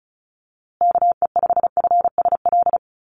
Morse code, heard as a keyed tone, about 34 wpm, 700 Hz. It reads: KE5FSL
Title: USA Call Signs Repeated 3x 35wpm